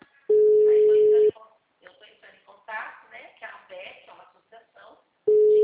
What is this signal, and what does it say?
Ringback tone (Europe, 425 Hz), with speech.
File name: out- -2001- - - .9072